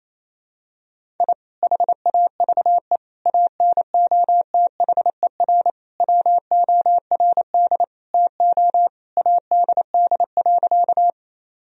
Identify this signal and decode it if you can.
Morse code, heard as a keyed tone, about 28 wpm, 700 Hz. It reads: I HAVE ANOTHER WORD TO ADD.
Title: Sentences from Top 1000 Words - Review Part 1 - 28wpm